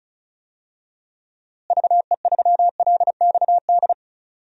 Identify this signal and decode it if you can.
Morse code, heard as a keyed tone, about 34 wpm, 700 Hz. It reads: VE3LXD
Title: DXCC-100 Call Signs - 6 Characters - Repeated 3x - 35wpm